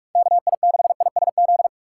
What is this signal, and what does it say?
Morse code, heard as a keyed tone, about 46 wpm, 700 Hz. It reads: KI6ISZ
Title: USA Call Signs - 6 Characters - 2x Character Spacing - 45wpm